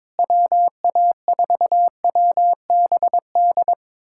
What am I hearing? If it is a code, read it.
Morse code, heard as a keyed tone, about 23 wpm, 700 Hz. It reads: WA4WBD
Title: USA Callsign DE Callsign - 6 Characters - 22wpm